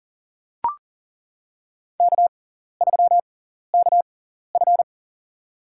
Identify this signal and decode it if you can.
Morse code, heard as a keyed tone, about 40 wpm, spaced as for 14 wpm, 700 Hz. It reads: K3KF
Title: USA Call Signs - 4 Characters - 6x Character Spacing - 40wpm